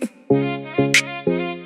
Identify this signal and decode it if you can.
There is no code.